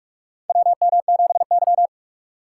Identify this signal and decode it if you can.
Morse code, heard as a keyed tone, about 44 wpm, 700 Hz. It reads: WM7Y
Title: USA Call Signs - 4 Characters - 6x Character Spacing - 45wpm